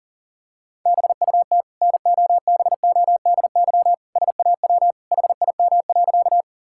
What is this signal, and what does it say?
Morse code, heard as a keyed tone, about 40 wpm, 700 Hz. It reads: BUT NOBODY SAW HIM.
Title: Sentences from Top 1600 Words 40wpm